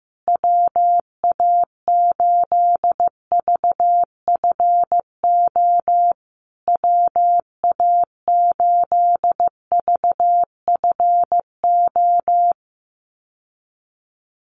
Morse code, heard as a keyed tone, about 15 wpm, 700 Hz. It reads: WA8VFO WA8VFO
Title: USA Call Signs Repeated 3x 15wpm